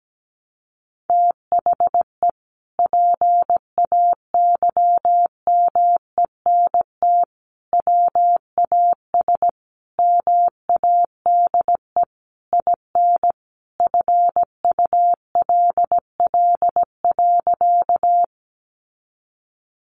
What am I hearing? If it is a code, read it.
Morse code, heard as a keyed tone, about 17 wpm, 700 Hz. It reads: THE PAYMENT WAS MADE IN FULL.
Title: Sentences from Top 1200 Words 17wpm